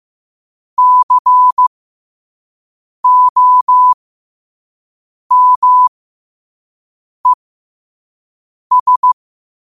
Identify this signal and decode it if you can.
Morse code, heard as a keyed tone, about 15 wpm, spaced as for 5 wpm, 1000 Hz. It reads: COMES
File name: qotdfiles / 5wpm